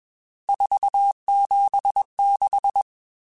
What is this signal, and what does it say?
Morse code, heard as a keyed tone, about 21 wpm, 800 Hz. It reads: 476
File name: RSTs / RST-mp3s / Various RST @20